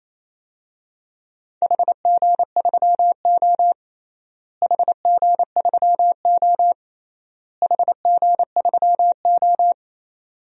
Morse code, heard as a keyed tone, about 28 wpm, 700 Hz. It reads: HG3O HG3O HG3O
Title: DXCC-100 Call Signs - 4 Characters - Repeated 3x - 28wpm